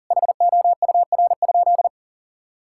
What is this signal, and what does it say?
Morse code, heard as a keyed tone, about 37 wpm, 700 Hz. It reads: HOUR?
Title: Sentences from Top 1000 Words - Review Part 5 - 40wpm